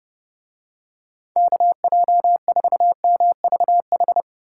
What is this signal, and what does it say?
Morse code, heard as a keyed tone, about 29 wpm, 700 Hz. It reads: KJ4MVH